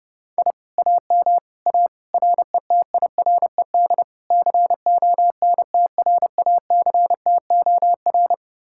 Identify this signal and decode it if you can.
Morse code, heard as a keyed tone, about 33 wpm, 700 Hz. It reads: I AM A RETIRED CONTRACTOR